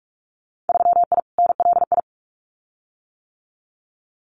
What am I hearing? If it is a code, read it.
Morse code, heard as a keyed tone, about 46 wpm, 700 Hz. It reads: 3I NLI
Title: Winter Field Day Exchanges 45wpm